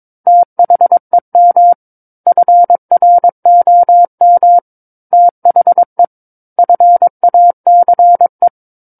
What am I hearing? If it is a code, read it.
Morse code, heard as a keyed tone, about 22 wpm, 700 Hz. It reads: THEM FROM THE FACE
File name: Book One / 13-Chapter 13 - 22 WPM